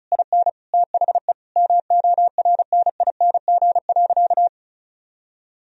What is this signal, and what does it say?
Morse code, heard as a keyed tone, about 36 wpm, 700 Hz. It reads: IN THE MORNING.
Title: Sentences from Top 1000 Words - Review Part 2 - 35wpm